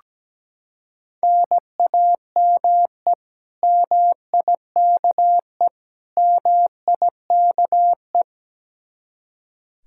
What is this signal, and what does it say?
Morse code, heard as a keyed tone, about 17 wpm, 700 Hz. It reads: NAME MIKE MIKE
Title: Practice QSO - Exchange 1 - Beginning - 17wpm